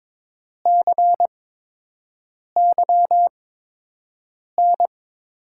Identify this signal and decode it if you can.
Morse code, heard as a keyed tone, about 22 wpm, spaced as for 6 wpm, 700 Hz. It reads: CYN
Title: USA Call Signs - 6 Characters - 8x Character Spacing - 22wpm